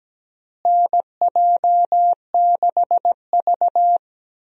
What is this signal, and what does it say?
Morse code, heard as a keyed tone, about 17 wpm, 700 Hz. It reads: NJ6V
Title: USA Call Signs - 4 Characters - Repeated 2x - 17wpm